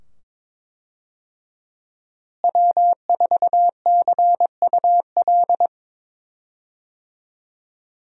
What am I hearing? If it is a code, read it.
Morse code, heard as a keyed tone, about 22 wpm, 700 Hz. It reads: W4CUL
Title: USA Call Signs - 5 Characters - Repeated 2x - 22wpm